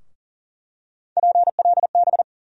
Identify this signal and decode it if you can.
Morse code, heard as a keyed tone, about 40 wpm, 700 Hz. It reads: PLB
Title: USA Call Sign Postfixes - 3 Letters 40wpm